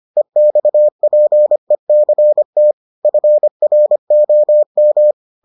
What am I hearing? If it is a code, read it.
Morse code, heard as a keyed tone, about 25 wpm, 600 Hz. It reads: EXPECT FROM